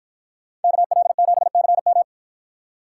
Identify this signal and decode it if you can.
Morse code, heard as a keyed tone, about 53 wpm, 700 Hz. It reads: KL7CM